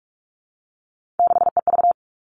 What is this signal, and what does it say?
Morse code, heard as a keyed tone, about 44 wpm, 700 Hz. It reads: 6EV
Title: All Cut Numbers - 3-Digit 45wpm